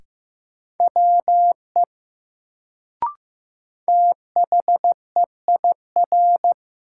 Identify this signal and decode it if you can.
Morse code, heard as a keyed tone, about 15 wpm, 700 Hz. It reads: WE THEIR
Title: Sets of 2 Words - New Word Drill - 15wpm